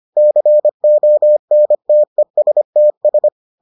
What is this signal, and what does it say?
Morse code, heard as a keyed tone, about 25 wpm, 600 Hz. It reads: CONTESTS